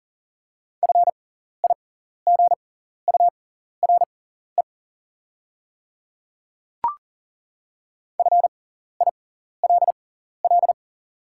Morse code, heard as a keyed tone, about 40 wpm, spaced as for 14 wpm, 700 Hz. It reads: FIGURE FILL
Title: 6x Character Spacing - Top 500 Words - 40wpm